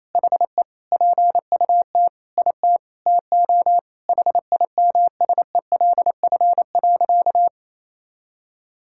Morse code, heard as a keyed tone, about 28 wpm, 700 Hz. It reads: HE PUT IT TO HIMSELF.
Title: Sentences from Top 1600 Words 28wpm